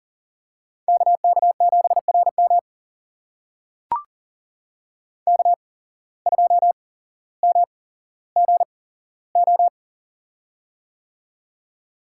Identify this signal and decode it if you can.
Morse code, heard as a keyed tone, about 38 wpm, spaced as for 17 wpm, 700 Hz. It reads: KK7RM K2MGO
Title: USA Call Signs - 5 Characters - 8x Character Spacing - 40wpm